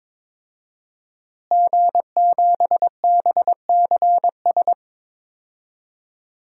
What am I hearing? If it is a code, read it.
Morse code, heard as a keyed tone, about 22 wpm, 700 Hz. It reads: G7BCS
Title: DXCC-100 Call Signs - 5 Characters - Repeated 3x - 22wpm